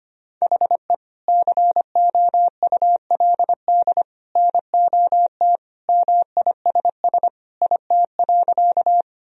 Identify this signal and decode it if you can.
Morse code, heard as a keyed tone, about 25 wpm, 700 Hz. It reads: HE COULD NOT MISS IT.
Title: Sentences from Top 1000 Words - Review Part 3 - 25wpm